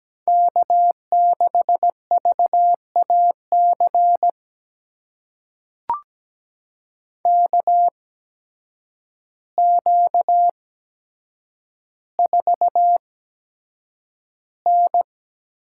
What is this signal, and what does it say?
Morse code, heard as a keyed tone, about 17 wpm, spaced as for 8 wpm, 700 Hz. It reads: K6VAC KQ4N
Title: USA Call Signs - 8x Character Spacing - 17wpm